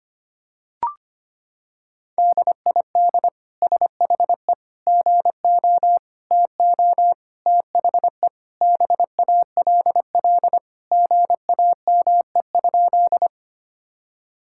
Morse code, heard as a keyed tone, about 25 wpm, 700 Hz. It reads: DID SHE GO TO THE BALL GAME?
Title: Sentences from Top 1000 Words - Review Part 6 - 25wpm